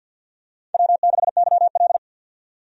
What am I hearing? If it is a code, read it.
Morse code, heard as a keyed tone, about 51 wpm, 700 Hz. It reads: W6YL
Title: USA Call Signs - 4 Characters - 50wpm